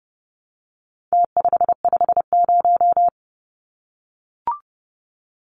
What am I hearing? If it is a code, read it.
Morse code, heard as a keyed tone, about 30 wpm, 700 Hz. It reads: T550
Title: Numbers - Contest Style - 4-Digit 30wpm